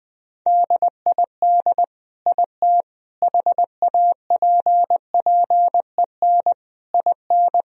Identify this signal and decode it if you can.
Morse code, heard as a keyed tone, about 21 wpm, 700 Hz. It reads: DID IT HAPPEN IN
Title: Sentences from Top 1000 Words - Review Part 2 - 20wpm